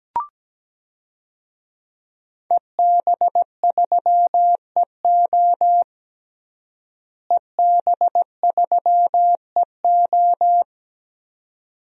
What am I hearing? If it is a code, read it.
Morse code, heard as a keyed tone, about 17 wpm, 700 Hz. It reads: EB3EO EB3EO
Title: DXCC-100 Call Signs - 5 Characters - Repeated 2x - 17wpm